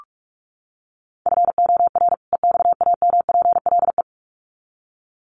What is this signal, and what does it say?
Morse code, heard as a keyed tone, about 44 wpm, 700 Hz. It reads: FOR EXAMPLE
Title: Sets of 2 Words 45wpm